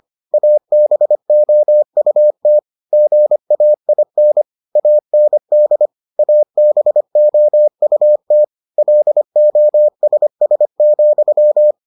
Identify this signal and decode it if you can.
Morse code, heard as a keyed tone, about 25 wpm, 600 Hz. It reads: ABOUT GAIN AND ABOUT LOSS,